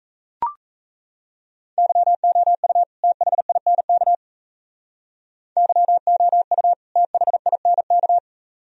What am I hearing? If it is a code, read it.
Morse code, heard as a keyed tone, about 34 wpm, 700 Hz. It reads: YOU THINK YOU THINK
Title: Speed-Racing - Sets of 2 Words 28wpm